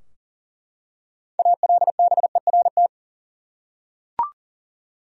Morse code, heard as a keyed tone, about 37 wpm, 700 Hz. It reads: ALBERT
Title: USA Common Names - 40wpm